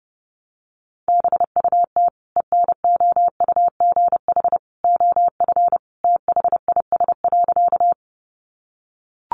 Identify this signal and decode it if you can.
Morse code, heard as a keyed tone, about 30 wpm, 700 Hz. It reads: BUT ENOUGH OF THIS.
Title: Sentences from Top 400 Words 30wpm